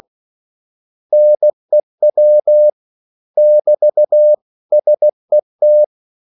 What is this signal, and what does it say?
Morse code, heard as a keyed tone, about 16 wpm, 600 Hz. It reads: NEW <BT> SET
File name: CW / Practice / 500MostCommonEnglishWords16